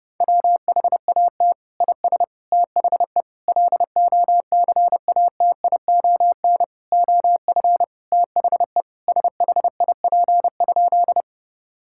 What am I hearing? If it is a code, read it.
Morse code, heard as a keyed tone, about 30 wpm, 700 Hz. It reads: WHAT IS THE LOCATION OF THE SHIP?
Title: Sentences from Top 1000 Words - Review Part 6 - 30wpm